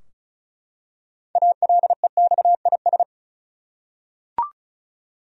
Morse code, heard as a keyed tone, about 34 wpm, 700 Hz. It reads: ALEXIS
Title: USA Common Names - 35wpm